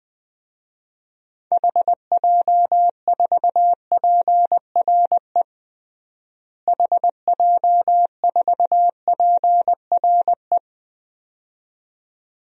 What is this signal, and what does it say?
Morse code, heard as a keyed tone, about 20 wpm, 700 Hz. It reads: HJ4PRE HJ4PRE
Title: DXCC-100 Call Signs - 6 Characters - Repeated 2x - 20wpm